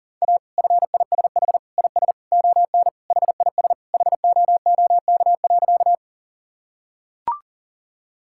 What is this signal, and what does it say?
Morse code, heard as a keyed tone, about 40 wpm, 700 Hz. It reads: A FISH IS ON HIS HOOK.
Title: Sentences from Top 1600 Words 40wpm